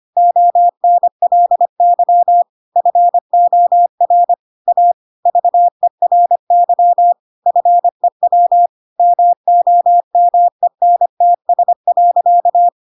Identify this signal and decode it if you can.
Morse code, heard as a keyed tone, about 25 wpm, 700 Hz. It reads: ONLY FOR A VERY FEW MOMENTS.
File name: Pinocchio / Pinocchio ch5 25 25 ik1hge